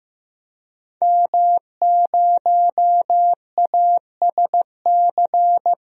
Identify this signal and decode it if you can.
Morse code, heard as a keyed tone, about 15 wpm, 700 Hz. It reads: M0ASC
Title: DXCC-100 Call Signs - 5 Characters - Repeated 2x - 15wpm